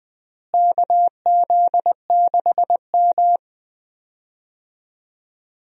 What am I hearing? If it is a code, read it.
Morse code, heard as a keyed tone, about 20 wpm, 700 Hz. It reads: KZ6M
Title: USA Call Signs - 4 Characters - Repeated 2x - 20wpm